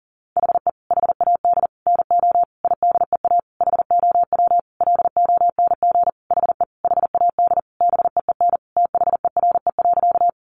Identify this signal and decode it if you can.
Morse code, heard as a keyed tone, about 40 wpm, 700 Hz. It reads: HE HAD NO IDEA HOW LONG HE HAD BEEN THERE.
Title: Sentences from Top 300 Words 40wpm